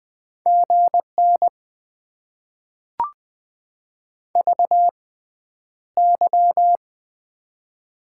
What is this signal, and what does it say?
Morse code, heard as a keyed tone, about 20 wpm, spaced as for 10 wpm, 700 Hz. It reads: GN VY
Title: Top 90 QSO Elements - 6x Character Spacing - 20wpm